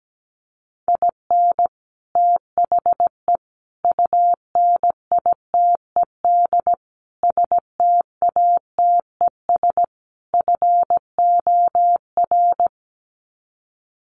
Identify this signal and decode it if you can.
Morse code, heard as a keyed tone, about 17 wpm, 700 Hz. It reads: IN THE UNITED STATES FOR
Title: Sets of 5 Words Encore2 17wpm